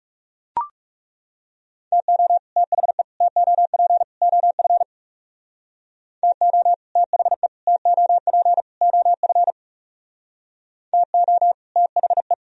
Morse code, heard as a keyed tone, about 38 wpm, 700 Hz. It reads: TO THE TOP OF TO THE TOP OF TO THE